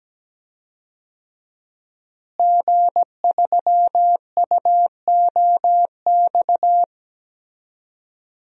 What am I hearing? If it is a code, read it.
Morse code, heard as a keyed tone, about 17 wpm, 700 Hz. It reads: G3UOX
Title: DXCC-100 Call Signs - 5 Characters - Repeated 2x - 17wpm